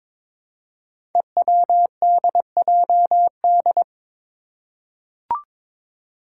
Morse code, heard as a keyed tone, about 22 wpm, 700 Hz. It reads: EWDJD
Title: Five Challenging Letter-Number Combinations 22wpm